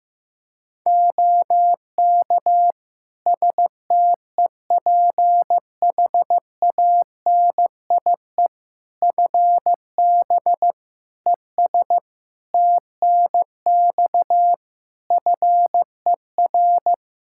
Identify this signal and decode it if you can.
Morse code, heard as a keyed tone, about 15 wpm, 700 Hz. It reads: OK STEPHANIE FB ES TNX FER
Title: Practice QSO - Exchange 2 - Info - 15wpm